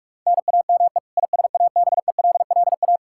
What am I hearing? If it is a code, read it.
Morse code, heard as a keyed tone, about 46 wpm, 700 Hz. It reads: NAME ISABELLA
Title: Practice QSO - Exchange 1 - Beginning - 45wpm